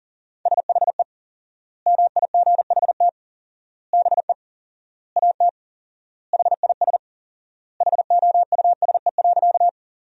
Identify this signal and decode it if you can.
Morse code, heard as a keyed tone, about 37 wpm, 700 Hz. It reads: SHE MIGHT BE AT HIS HOUSE.
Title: 4x Word-Spacing - Sentences from Top 300 Words - 40wpm